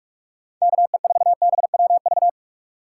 Morse code, heard as a keyed tone, about 46 wpm, 700 Hz. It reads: KE4BWV
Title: USA Call Signs - 6 Characters - 3x Character Spacing - 45wpm